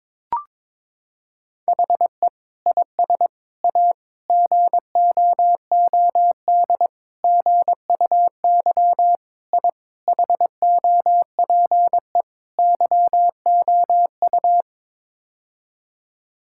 Morse code, heard as a keyed tone, about 21 wpm, 700 Hz. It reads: HE IS A GOOD GUY I HOPE YOU
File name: practice / 22wpm / Sets of 8 Words - 22wpm